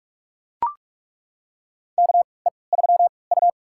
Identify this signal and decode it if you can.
Morse code, heard as a keyed tone, about 42 wpm, spaced as for 25 wpm, 700 Hz. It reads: KE3U
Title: USA Call Signs - 4 Characters - 3x Character Spacing - 45wpm